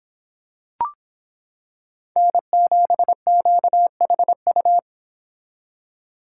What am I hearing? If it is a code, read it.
Morse code, heard as a keyed tone, about 25 wpm, 700 Hz. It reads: N7QHU